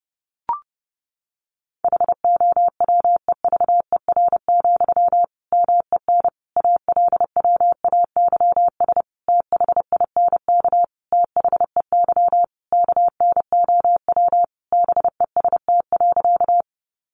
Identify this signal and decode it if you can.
Morse code, heard as a keyed tone, about 30 wpm, 700 Hz. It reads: HOWEVER, MEN ALWAYS THINK THEY KNOW BEST.
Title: Sentences from Top 600 Words 30wpm